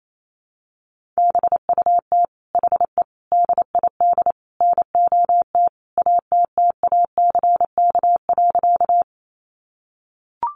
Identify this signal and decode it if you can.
Morse code, heard as a keyed tone, about 28 wpm, 700 Hz. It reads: BUT HE DID NOT ATTACK.
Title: Sentences from Top 1500 Words 28wpm